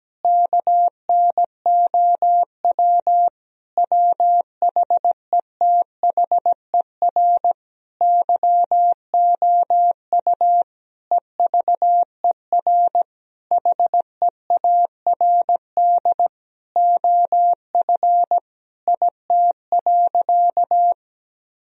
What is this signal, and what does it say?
Morse code, heard as a keyed tone, about 17 wpm, 700 Hz. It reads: KNOW WHETHER YOU EVER HEARD OF IT.
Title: Sentences from Top 1000 Words - Review Part 3 - 17wpm